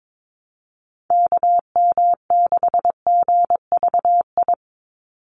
Morse code, heard as a keyed tone, about 22 wpm, 700 Hz. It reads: KM6GVI
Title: USA Call Signs 22wpm